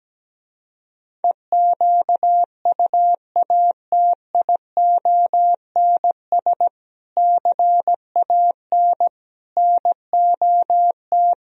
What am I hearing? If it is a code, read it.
Morse code, heard as a keyed tone, about 17 wpm, 700 Hz. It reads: EQUATIONS CAN NOT
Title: Speed-Racing - Sets of 5 Words 17wpm